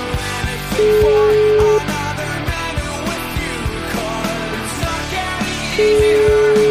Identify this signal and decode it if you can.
Ringback tone (Europe, 425 Hz), with noise.